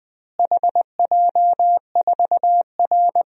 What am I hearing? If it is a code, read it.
Morse code, heard as a keyed tone, about 20 wpm, 700 Hz. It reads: HJ4R
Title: DXCC-100 Call Signs - 4 Characters - Repeated 2x - 20wpm